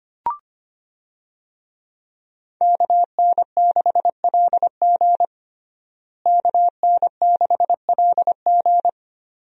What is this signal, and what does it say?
Morse code, heard as a keyed tone, about 25 wpm, 700 Hz. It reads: KN6LG KN6LG
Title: USA Call Signs - 5 Characters - Repeated 2x - 25wpm